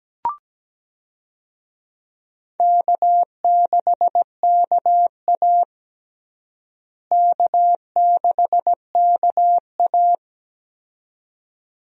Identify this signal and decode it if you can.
Morse code, heard as a keyed tone, about 17 wpm, 700 Hz. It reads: K6KA K6KA
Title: USA Call Signs - 4 Characters - Repeated 2x - 17wpm